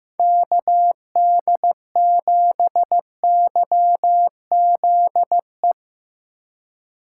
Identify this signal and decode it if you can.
Morse code, heard as a keyed tone, about 17 wpm, 700 Hz. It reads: KD7YZE
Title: Speed Racing - USA Call Signs 15wpm